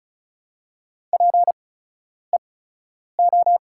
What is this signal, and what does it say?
Morse code, heard as a keyed tone, about 34 wpm, spaced as for 10 wpm, 690 Hz. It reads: PEO
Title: 8x Character Spacing - Top 100 Words - 35wpm